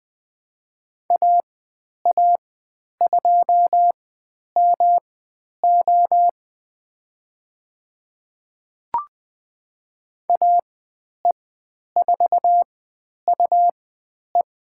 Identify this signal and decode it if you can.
Morse code, heard as a keyed tone, about 20 wpm, spaced as for 10 wpm, 700 Hz. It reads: AA2MO AE4UE
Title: 109 Lesson 18g - Letter m - Call Signs 20wpm